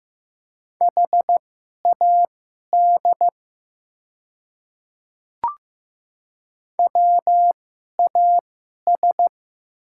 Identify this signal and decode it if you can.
Morse code, heard as a keyed tone, about 15 wpm, spaced as for 11 wpm, 700 Hz. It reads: HAD WAS
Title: 2x Character Spacing - Top 200 Words - 15wpm